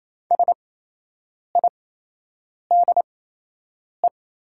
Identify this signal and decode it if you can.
Morse code, heard as a keyed tone, about 28 wpm, spaced as for 8 wpm, 700 Hz. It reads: SIDE
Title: 8x Character Spacing - Top 200 Words - 28wpm